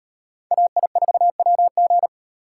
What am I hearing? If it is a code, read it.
Morse code, heard as a keyed tone, about 38 wpm, 700 Hz. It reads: AI4WG